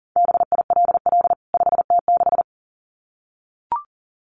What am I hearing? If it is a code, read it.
Morse code, heard as a keyed tone, about 40 wpm, 700 Hz. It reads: BILL 5T6
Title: CWT Exchange 40wpm